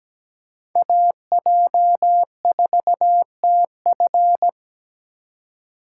Morse code, heard as a keyed tone, about 17 wpm, 700 Hz. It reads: AJ4TF
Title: K1USN SST QSO Practice - 17wpm